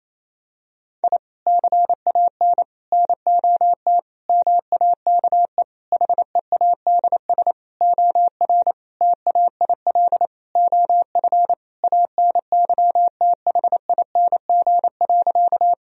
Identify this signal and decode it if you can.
Morse code, heard as a keyed tone, about 28 wpm, 700 Hz. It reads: I CAN NOT MAKE HEADS OR TAIL OF ANYTHING.
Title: Sentences from Top 1000 Words - Review Part 3 - 28wpm